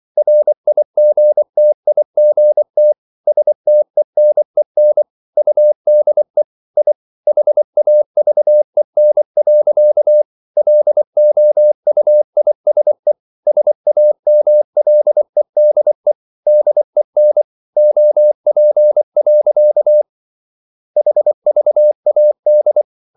Morse code, heard as a keyed tone, about 24 wpm, 600 Hz. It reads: RIGTIGT STENEN UDE I HAVEN. LOUISE SAMLEDE DEN OP. HVAD
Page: Ferie 24 wpm